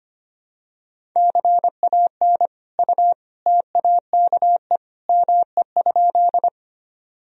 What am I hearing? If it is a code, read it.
Morse code, heard as a keyed tone, about 25 wpm, 700 Hz. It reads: CAN U TAKE ME?
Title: Sentences from Top 200 Words and Abbreviations 25wpm